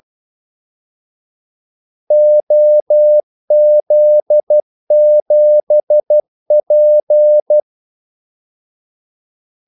Morse code, heard as a keyed tone, about 12 wpm, 600 Hz. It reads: OZ7P
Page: 2 ekstra ordmellemrum mellem kaldesignalerne